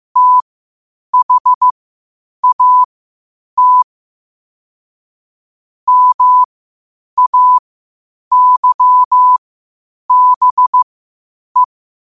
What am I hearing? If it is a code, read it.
Morse code, heard as a keyed tone, about 15 wpm, spaced as for 8 wpm, 1000 Hz. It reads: THAT MAYBE
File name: qotdfiles / 8wpm